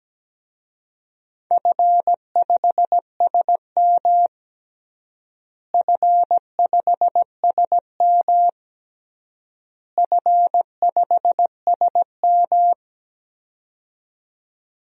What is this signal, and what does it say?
Morse code, heard as a keyed tone, about 17 wpm, 700 Hz. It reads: F5SM F5SM F5SM
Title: DXCC-100 Call Signs - 4 Characters - Repeated 3x - 17wpm